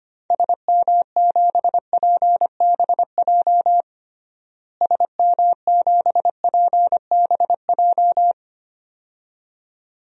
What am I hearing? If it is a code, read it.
Morse code, heard as a keyed tone, about 25 wpm, 700 Hz. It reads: SM7PBJ SM7PBJ
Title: DXCC-100 Call Signs - 6 Characters - Repeated 2x - 25wpm